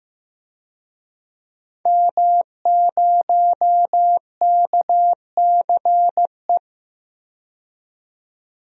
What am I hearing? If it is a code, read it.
Morse code, heard as a keyed tone, about 16 wpm, 700 Hz. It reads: M0KCE
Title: DXCC-100 Call Signs - 5 Characters - 15wpm